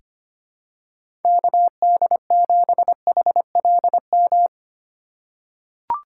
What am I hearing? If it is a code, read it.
Morse code, heard as a keyed tone, about 25 wpm, 700 Hz. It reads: KD7HLM
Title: USA Call Signs - 6 Characters - 4x Character Spacing - 25wpm